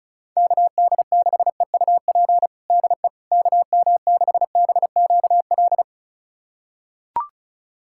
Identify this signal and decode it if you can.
Morse code, heard as a keyed tone, about 34 wpm, 690 Hz. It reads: KD6EUP DE KM6BQL
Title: USA Callsign DE Callsign - 6 Characters - 35wpm